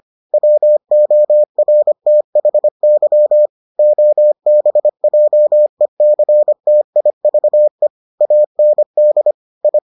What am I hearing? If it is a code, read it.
Morse code, heard as a keyed tone, about 25 wpm, 600 Hz. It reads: WORTHY OBJECTIVE AND I